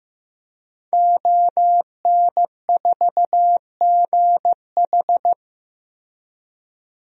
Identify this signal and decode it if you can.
Morse code, heard as a keyed tone, about 15 wpm, 700 Hz. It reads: ON4GH
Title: DXCC-100 Call Signs - 5 Characters - 15wpm